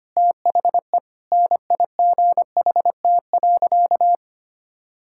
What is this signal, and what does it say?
Morse code, heard as a keyed tone, about 25 wpm, 700 Hz. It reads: THE NIGHT.
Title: Sentences from Top 1600 Words 25wpm